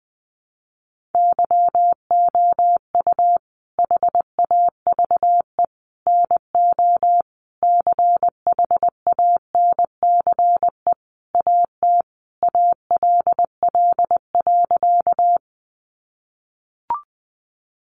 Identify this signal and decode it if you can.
Morse code, heard as a keyed tone, about 20 wpm, 700 Hz. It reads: YOU HAVE NO CHANCE AT ALL.
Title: Sentences from Top 800 Words 20wpm